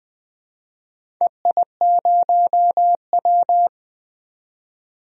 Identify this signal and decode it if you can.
Morse code, heard as a keyed tone, about 20 wpm, 700 Hz. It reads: EI0W
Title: DXCC-100 Call Signs - 4 Characters - 20wpm